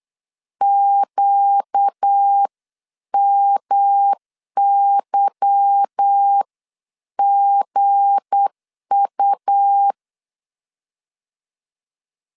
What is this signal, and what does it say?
Morse code, heard as a keyed tone, about 8 wpm, spaced as for 7 wpm, 800 Hz. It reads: QMYGU
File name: LEZIONI / 9 2 8wpm 3437